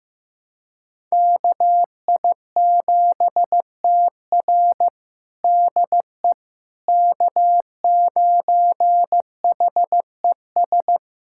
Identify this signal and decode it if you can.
Morse code, heard as a keyed tone, about 15 wpm, 700 Hz. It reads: KI7TR DE K9HES
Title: USA Callsign DE Callsign - 5 Characters - 15wpm